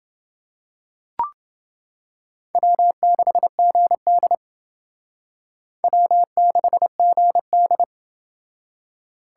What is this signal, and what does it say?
Morse code, heard as a keyed tone, about 30 wpm, 700 Hz. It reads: W6GD W6GD